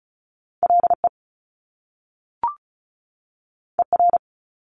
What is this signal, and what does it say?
Morse code, heard as a keyed tone, about 36 wpm, 700 Hz. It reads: LE ER
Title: Top Two Letter Combinations in Words 35wpm